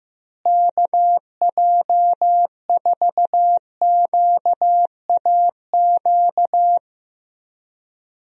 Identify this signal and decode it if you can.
Morse code, heard as a keyed tone, about 15 wpm, 700 Hz. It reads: KJ4QAQ
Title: USA Call Signs - 6 Characters - Repeated 2x - 15wpm